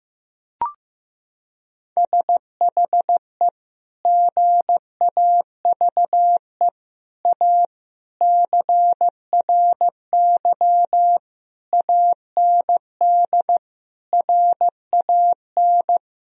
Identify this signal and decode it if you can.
Morse code, heard as a keyed tone, about 15 wpm, 700 Hz. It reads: SHE GAVE A CRY AND RAN
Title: Sentences from Top 1000 Words - Review Part 3 - 15wpm